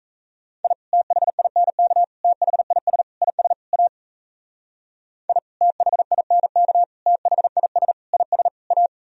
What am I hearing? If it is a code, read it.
Morse code, heard as a keyed tone, about 34 wpm, 700 Hz. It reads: I THINK THIS IS A I THINK THIS IS A
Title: Speed-Racing - Sets of 5 Words 28wpm